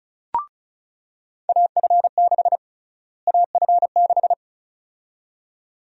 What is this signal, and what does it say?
Morse code, heard as a keyed tone, about 34 wpm, 700 Hz. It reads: AF6 AF6
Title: USA Call Sign Prefixes - Repeated 2x - 35wpm